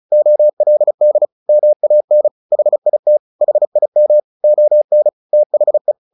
Morse code, heard as a keyed tone, about 34 wpm, 600 Hz. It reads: OLD MAN HIT HIM ON THE
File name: the old man and the sea / the old man and the sea 1 35wpm